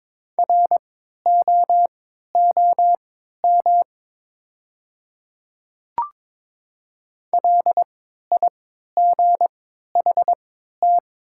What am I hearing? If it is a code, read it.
Morse code, heard as a keyed tone, about 22 wpm, spaced as for 12 wpm, 700 Hz. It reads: ROOM LIGHT
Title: 3x Character Spacing - Top 300 Words - 22wpm